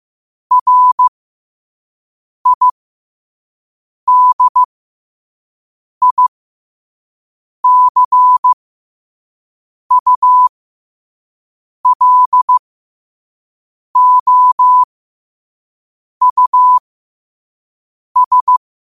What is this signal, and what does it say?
Morse code, heard as a keyed tone, about 15 wpm, spaced as for 5 wpm, 1000 Hz. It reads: RIDICULOUS
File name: qotdfiles / 5wpm